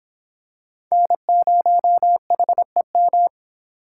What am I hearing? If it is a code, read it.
Morse code, heard as a keyed tone, about 23 wpm, 700 Hz. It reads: N0HEM